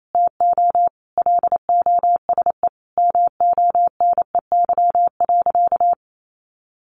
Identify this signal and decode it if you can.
Morse code, heard as a keyed tone, about 28 wpm, 700 Hz. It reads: TO LOSE MONEY.
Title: Sentences from Top 1100 Words 28wpm